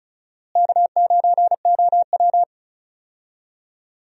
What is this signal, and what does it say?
Morse code, heard as a keyed tone, about 34 wpm, 700 Hz. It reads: K9OW
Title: USA Call Signs - 4 Characters - Repeated 3x - 35wpm